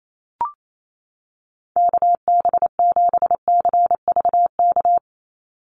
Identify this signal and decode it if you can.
Morse code, heard as a keyed tone, about 28 wpm, 700 Hz. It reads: KB7CVK
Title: USA Call Signs 28wpm